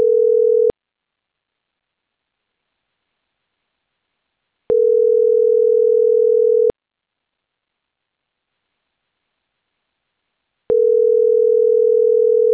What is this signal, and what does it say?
Ringback tone (North America), 440 + 480 Hz.